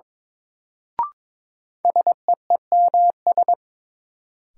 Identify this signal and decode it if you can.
Morse code, heard as a keyed tone, about 22 wpm, 700 Hz. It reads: SEEMS
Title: Sets of 3 Words - New Word Drill - 22wpm